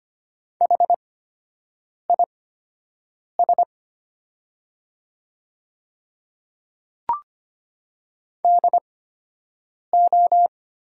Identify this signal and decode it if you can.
Morse code, heard as a keyed tone, about 25 wpm, spaced as for 7 wpm, 700 Hz. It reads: HIS DO
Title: 8x Character Spacing - Top 100 Words - 25wpm